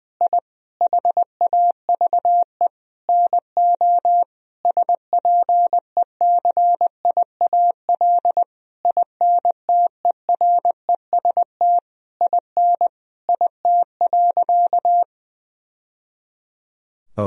Morse code, heard as a keyed tone, about 20 wpm, 700 Hz. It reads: I HAVE NO SPECIAL INTEREST IN IT.
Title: Sentences from Top 1000 Words - Review Part 3 - 20wpm